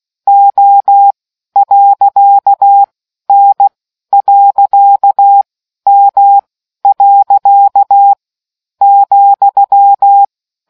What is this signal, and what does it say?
Morse code, heard as a keyed tone, about 16 wpm, 800 Hz. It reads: O . N . M . ,